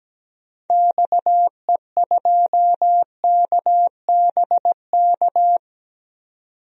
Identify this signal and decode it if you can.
Morse code, heard as a keyed tone, about 17 wpm, 700 Hz. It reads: XE2KBK
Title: DXCC-100 Call Signs - 6 Characters - 17wpm